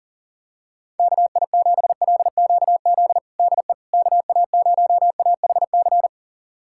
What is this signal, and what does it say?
Morse code, heard as a keyed tone, about 40 wpm, 700 Hz. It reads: KI7LQZ DE KA0AHC
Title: USA Callsign DE Callsign 40wpm